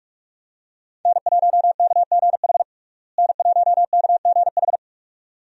Morse code, heard as a keyed tone, about 46 wpm, 700 Hz. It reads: N1KGH N1KGH
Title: USA Call Signs - 5 Characters - Repeated 2x - 45wpm